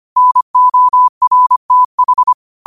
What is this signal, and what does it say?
Morse code, heard as a keyed tone, about 25 wpm, 1000 Hz. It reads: NORTH
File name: qotdfiles / 25wpm